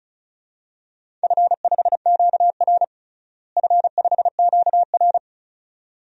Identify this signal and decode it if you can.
Morse code, heard as a keyed tone, about 36 wpm, 700 Hz. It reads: F5QR F5QR
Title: DXCC-100 Call Signs - 4 Characters - Repeated 2x - 35wpm